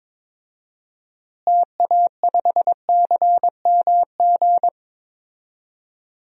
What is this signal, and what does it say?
Morse code, heard as a keyed tone, about 21 wpm, 700 Hz. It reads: TA5CMG
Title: DXCC-100 Call Signs - 6 Characters - Repeated 2x - 22wpm